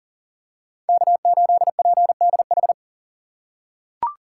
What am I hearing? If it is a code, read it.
Morse code, heard as a keyed tone, about 37 wpm, 700 Hz. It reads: K8PDH
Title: USA Call Signs - 3x Character Spacing - 40wpm